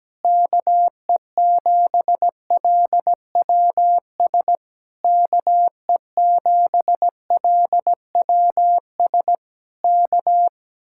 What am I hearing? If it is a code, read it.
Morse code, heard as a keyed tone, about 17 wpm, 700 Hz. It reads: KE7LWS KE7LWS K
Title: Speed-Racing Common QSO Phrases - 17wpm